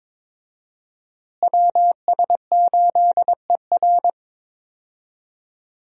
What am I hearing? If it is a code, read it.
Morse code, heard as a keyed tone, about 22 wpm, 700 Hz. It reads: WS8ER
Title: USA Call Signs - 5 Characters - Repeated 3x - 22wpm